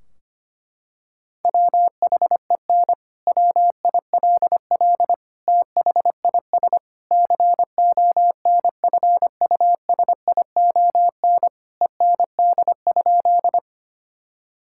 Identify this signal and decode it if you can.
Morse code, heard as a keyed tone, about 25 wpm, 700 Hz. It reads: WHEN WILL THIS CONFUSION END?
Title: Sentences from Top 1600 Words 25wpm